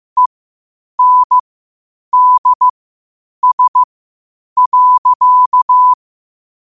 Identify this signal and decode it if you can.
Morse code, heard as a keyed tone, about 15 wpm, spaced as for 8 wpm, 1000 Hz. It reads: ENDS.
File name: qotdfiles / 8wpm